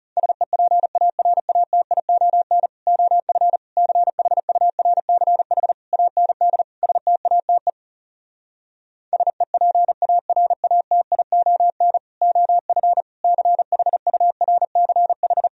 Morse code, heard as a keyed tone, about 37 wpm, 700 Hz. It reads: SEPARATION OF CHURCH AND STATE SEPARATION OF CHURCH
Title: Speed-Racing - Sets of 5 Words 30wpm